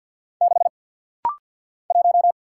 Morse code, heard as a keyed tone, about 50 wpm, 690 Hz. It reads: 6 1
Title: Single Letter-Number Alt - Rapid-Fire 50wpm